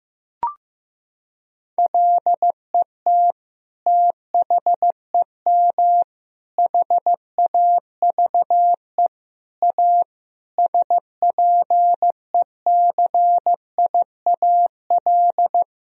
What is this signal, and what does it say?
Morse code, heard as a keyed tone, about 15 wpm, 700 Hz. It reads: LET THEM HAVE A SPECIAL
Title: Sentences from Top 1000 Words - Review Part 3 - 15wpm